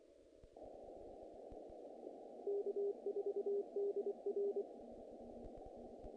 Morse code, heard as a keyed tone, about 25 wpm, 400 Hz. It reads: K4DR
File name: contests / naqp-cw-k5tr-op-k5pi- -1800